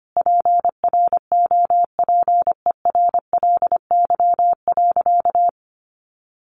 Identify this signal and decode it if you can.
Morse code, heard as a keyed tone, about 25 wpm, 700 Hz. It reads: PROPERLY.
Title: Sentences from Top 1100 Words 25wpm